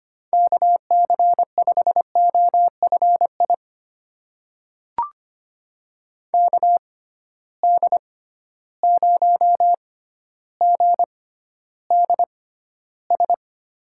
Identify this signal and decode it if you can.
Morse code, heard as a keyed tone, about 25 wpm, spaced as for 13 wpm, 700 Hz. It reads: KC5OFI KD0GDS
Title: USA Call Signs - 6 Characters - 6x Character Spacing - 25wpm